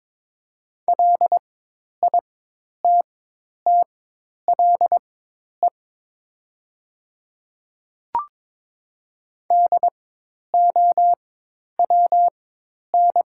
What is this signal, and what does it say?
Morse code, heard as a keyed tone, about 22 wpm, spaced as for 10 wpm, 700 Hz. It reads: LITTLE DOWN
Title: 4x Character Spacing - Top 300 Words - 22wpm